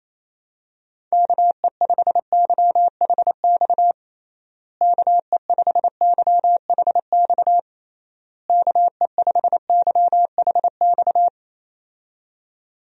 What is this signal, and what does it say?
Morse code, heard as a keyed tone, about 27 wpm, 700 Hz. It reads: KE5YHX KE5YHX KE5YHX
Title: USA Call Signs - 6 Characters - Repeated 3x - 28wpm